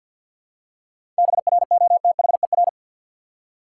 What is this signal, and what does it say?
Morse code, heard as a keyed tone, about 53 wpm, 700 Hz. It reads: BROTHER
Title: Seven Letter Words - 50wpm